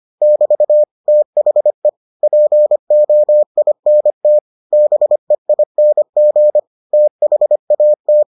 Morse code, heard as a keyed tone, about 25 wpm, 600 Hz. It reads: <BT> THE POINT BEING THAT